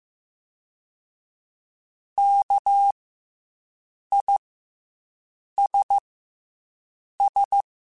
Morse code, heard as a keyed tone, about 15 wpm, 800 Hz. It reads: K I S S
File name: Special INGs / Special INGs-mp3s / SlowToFaster ing @13